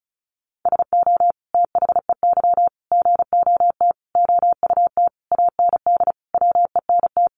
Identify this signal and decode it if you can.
Morse code, heard as a keyed tone, about 34 wpm, 700 Hz. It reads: SO THEY GOT OUT AND WENT
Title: Sentences from Top 300 Words 35wpm